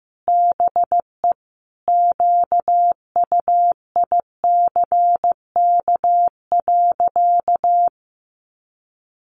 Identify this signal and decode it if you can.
Morse code, heard as a keyed tone, about 15 wpm, 700 Hz. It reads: BE QUICK.
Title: Sentences from Top 600 Words 15wpm